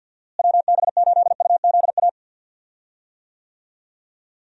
Morse code, heard as a keyed tone, about 50 wpm, 700 Hz. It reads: WB8UZA
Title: USA Call Signs - 6 Characters - Repeated 2x - 50wpm